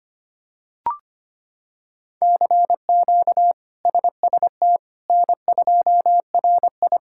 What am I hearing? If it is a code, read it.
Morse code, heard as a keyed tone, about 25 wpm, 700 Hz. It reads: CQ SST N2RI
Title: K1USN SST QSO Practice - 25wpm